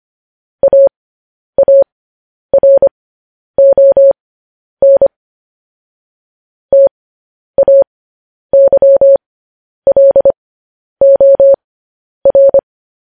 Morse code, heard as a keyed tone, about 25 wpm, spaced as for 10 wpm, 550 Hz. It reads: AARON TAYLOR